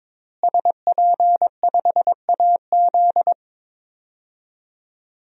Morse code, heard as a keyed tone, about 22 wpm, 700 Hz. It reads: SP5AZ
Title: DXCC-100 Call Signs - 5 Characters - Repeated 3x - 22wpm